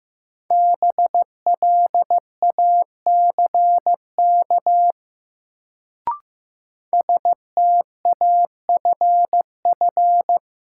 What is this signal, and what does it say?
Morse code, heard as a keyed tone, about 15 wpm, 700 Hz. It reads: BLACK STAFF
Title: Sets of 5 Words Encore - New Word Drill - 15wpm